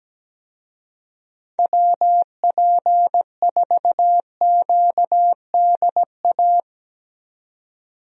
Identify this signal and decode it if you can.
Morse code, heard as a keyed tone, about 17 wpm, 700 Hz. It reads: WP4QDA
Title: USA Call Signs - 6 Characters - 17wpm